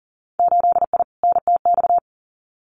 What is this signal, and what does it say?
Morse code, heard as a keyed tone, about 41 wpm, 700 Hz. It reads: 8I NTX
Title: Winter Field Day Exchanges Encore 40wpm